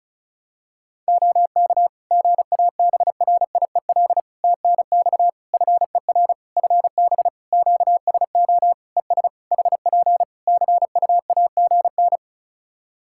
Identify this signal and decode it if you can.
Morse code, heard as a keyed tone, about 36 wpm, 700 Hz. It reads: OK GABRIEL TNX FER FB QSO ES HP CUAGN
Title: Practice QSO - Exchange 4 - Ending - 35wpm